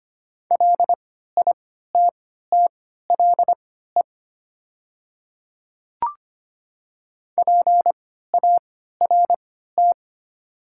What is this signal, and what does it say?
Morse code, heard as a keyed tone, about 25 wpm, spaced as for 14 wpm, 690 Hz. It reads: LITTLE PART
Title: 3x Character Spacing - Top 200 Words - 25wpm